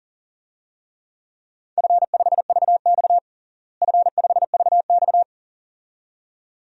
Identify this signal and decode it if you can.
Morse code, heard as a keyed tone, about 40 wpm, 700 Hz. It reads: F5VX F5VX
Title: DXCC-100 Call Signs - 4 Characters - Repeated 2x - 40wpm